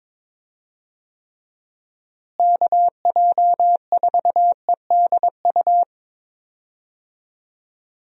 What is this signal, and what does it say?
Morse code, heard as a keyed tone, about 22 wpm, 700 Hz. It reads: KJ4EDU
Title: USA Call Signs - 6 Characters - Repeated 3x - 22wpm